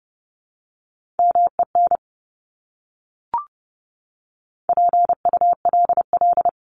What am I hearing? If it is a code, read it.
Morse code, heard as a keyed tone, about 30 wpm, 700 Hz. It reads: MEN PULL
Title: Top 500 Words Repeated 30wpm